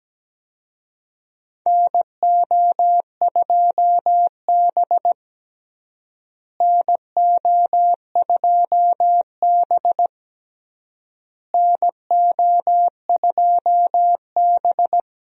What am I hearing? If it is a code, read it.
Morse code, heard as a keyed tone, about 17 wpm, 700 Hz. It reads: NO2B NO2B NO2B
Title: USA Call Signs - 4 Characters - Repeated 3x - 17wpm